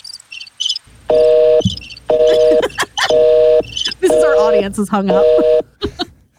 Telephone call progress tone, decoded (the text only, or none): busy tone